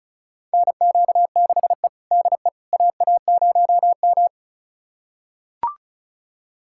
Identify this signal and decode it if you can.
Morse code, heard as a keyed tone, about 36 wpm, 700 Hz. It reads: NQ6E DE AA0M
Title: USA Callsign DE Callsign - 4 Characters - 35wpm